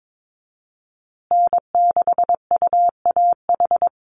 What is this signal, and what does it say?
Morse code, heard as a keyed tone, about 22 wpm, 700 Hz. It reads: N6UAH
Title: USA Call Signs Repeated 2x 22wpm